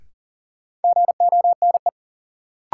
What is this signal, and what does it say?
Morse code, heard as a keyed tone, about 40 wpm, 700 Hz. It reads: GONE
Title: Sentences from Top 1200 Words - New Word Drill - 40wpm